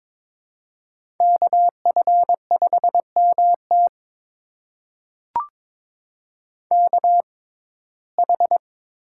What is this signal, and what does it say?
Morse code, heard as a keyed tone, about 23 wpm, spaced as for 16 wpm, 700 Hz. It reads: KF5MT KH
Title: USA Call Signs - 5 Characters - 6x Character Spacing - 22wpm